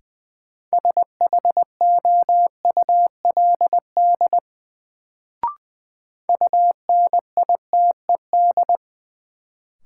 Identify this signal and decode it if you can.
Morse code, heard as a keyed tone, about 20 wpm, 700 Hz. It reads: SHOULD UNITED
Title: Sets of 2 Words - New Word Drill - 20wpm